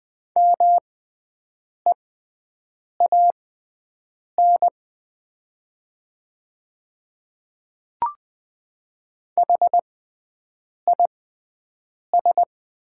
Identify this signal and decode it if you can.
Morse code, heard as a keyed tone, about 20 wpm, spaced as for 7 wpm, 700 Hz. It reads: MEAN HIS
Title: 6x Character Spacing - Top 200 Words - 20wpm